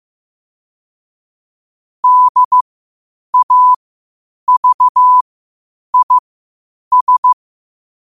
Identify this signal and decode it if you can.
Morse code, heard as a keyed tone, about 15 wpm, spaced as for 8 wpm, 1000 Hz. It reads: DAVIS
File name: qotdfiles / 8wpm